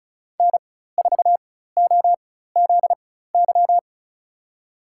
Morse code, heard as a keyed tone, about 36 wpm, spaced as for 16 wpm, 700 Hz. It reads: N4OZY
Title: USA Call Signs - 5 Characters - 4x Character Spacing - 35wpm